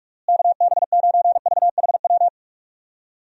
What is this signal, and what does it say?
Morse code, heard as a keyed tone, about 42 wpm, 700 Hz. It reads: KB9VHW
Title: USA Call Signs - 6 Characters - Repeated 2x - 45wpm